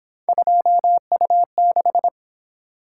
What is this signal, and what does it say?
Morse code, heard as a keyed tone, about 27 wpm, 700 Hz. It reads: 2U6
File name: Speed-Racing - Three Letter-Number Combinations - 22wpm